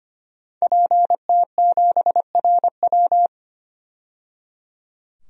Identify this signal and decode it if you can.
Morse code, heard as a keyed tone, about 25 wpm, 700 Hz. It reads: PT7RW
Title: DXCC-100 Call Signs - 5 Characters - 25wpm